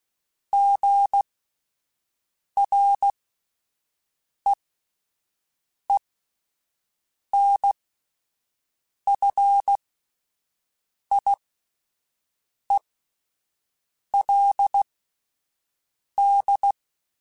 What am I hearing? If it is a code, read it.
Morse code, heard as a keyed tone, about 16 wpm, spaced as for 5 wpm, 800 Hz. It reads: GREENFIELD
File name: fromSMs / FromSMs mp3s / Cities@5